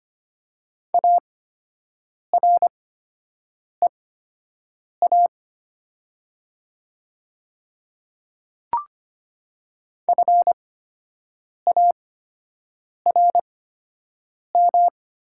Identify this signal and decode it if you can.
Morse code, heard as a keyed tone, about 25 wpm, spaced as for 7 wpm, 700 Hz. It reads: AREA FARM
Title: 8x Character Spacing - Top 500 Words - 25wpm